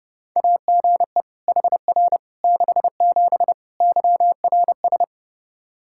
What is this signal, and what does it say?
Morse code, heard as a keyed tone, about 32 wpm, 700 Hz. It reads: AGE HR 67 YRS
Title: K1USN SST QSO Practice 30wpm